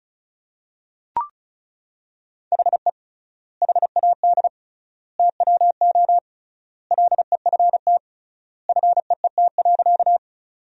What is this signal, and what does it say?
Morse code, heard as a keyed tone, about 36 wpm, 700 Hz. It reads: HE HAD TWO LEFT FEET.
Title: 3x Word-Spacing - Sentences from Top 300 Words - 35wpm